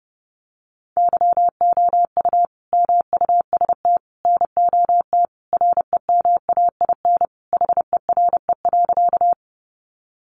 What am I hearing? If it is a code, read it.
Morse code, heard as a keyed tone, about 30 wpm, 700 Hz. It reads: YOU MUST NOT REMAIN HERE.
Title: Sentences from Top 1200 Words 30wpm